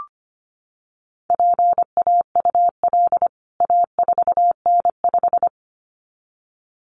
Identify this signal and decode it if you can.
Morse code, heard as a keyed tone, about 25 wpm, 690 Hz. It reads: PAUL A4N5
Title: CWT Exchange 25wpm